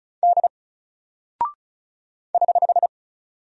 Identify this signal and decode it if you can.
Morse code, heard as a keyed tone, about 35 wpm, 700 Hz. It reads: D <HH>
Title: Single Character (Letter:Number:Punctuation:Prosign) 35wpm